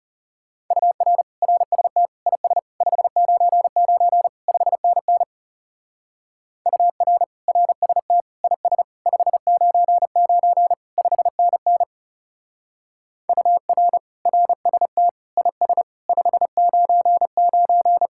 Morse code, heard as a keyed tone, about 35 wpm, 700 Hz. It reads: UR RST IS 599 5NN UR RST IS 599 5NN UR RST IS 599
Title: Speed-Racing Common QSO Phrases - 30wpm